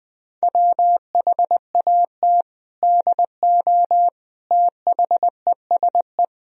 Morse code, heard as a keyed tone, about 21 wpm, 700 Hz. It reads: WHAT DO THESE
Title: Sentences from Top 1000 Words - Review Part 1 - 20wpm